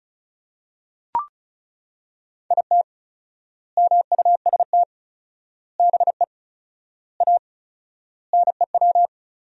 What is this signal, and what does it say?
Morse code, heard as a keyed tone, about 34 wpm, 700 Hz. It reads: IT MUST BE A NEW
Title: 4x Word-Spacing - Sentences from Top 500 Words - 35wpm